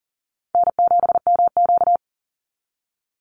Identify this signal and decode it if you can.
Morse code, heard as a keyed tone, about 40 wpm, 700 Hz. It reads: N7MQ
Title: Easy Call Signs - 40wpm